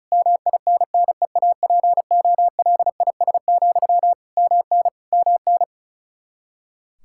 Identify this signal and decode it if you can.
Morse code, heard as a keyed tone, about 36 wpm, 700 Hz. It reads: MINNEAPOLIS, MN MN
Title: Practice QSO - Exchange 1 - Beginning - 35wpm